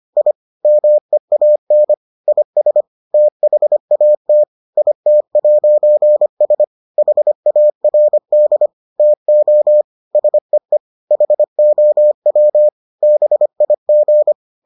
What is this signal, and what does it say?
Morse code, heard as a keyed tone, about 25 wpm, 600 Hz. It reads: I MEAN IS THAT IT'S HARD TO SEE HOW BIG